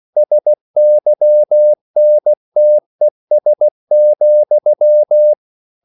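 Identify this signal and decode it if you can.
Morse code, heard as a keyed tone, about 16 wpm, 600 Hz. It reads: SYNTES,
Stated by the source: Slow (QRS) Morse code at 16 wpm, from CW med Gnister.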